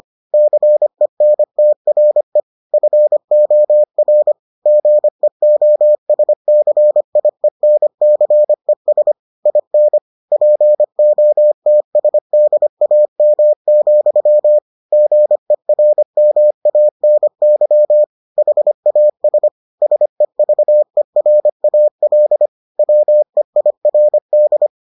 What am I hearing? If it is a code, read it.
Morse code, heard as a keyed tone, about 25 wpm, 600 Hz. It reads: CENTRE FOR GEOSCIENCES IN POTSDAM, GERMANY HAS SEVERAL WEIRD